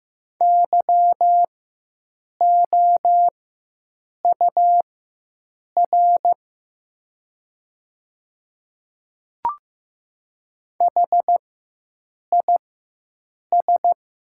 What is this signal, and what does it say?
Morse code, heard as a keyed tone, about 15 wpm, spaced as for 7 wpm, 700 Hz. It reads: YOUR HIS
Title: 4x Character Spacing - Top 100 Words - 15wpm